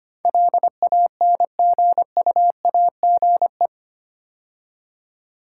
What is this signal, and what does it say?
Morse code, heard as a keyed tone, about 25 wpm, 700 Hz. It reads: LANGUAGE
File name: 8 Letter Words - First 3 Letters Given - 25wpm